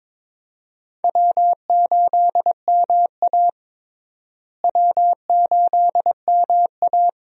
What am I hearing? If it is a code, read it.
Morse code, heard as a keyed tone, about 22 wpm, 700 Hz. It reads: W8MA W8MA
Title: USA Call Signs - 4 Characters - Repeated 2x - 22wpm